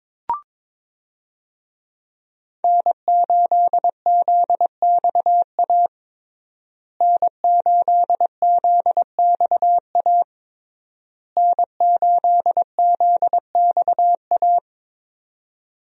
Morse code, heard as a keyed tone, about 22 wpm, 700 Hz. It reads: N8ZXA N8ZXA N8ZXA
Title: USA Call Signs - 5 Characters - Repeated 3x - 22wpm